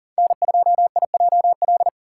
Morse code, heard as a keyed tone, about 40 wpm, 700 Hz. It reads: N2IJL
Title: USA Call Signs - 5 Characters - 3x Character Spacing - 40wpm